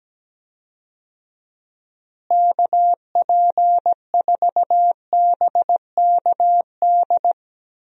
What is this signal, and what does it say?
Morse code, heard as a keyed tone, about 17 wpm, 700 Hz. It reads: KP4BKD
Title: DXCC-100 Call Signs - 6 Characters - Repeated 3x - 17wpm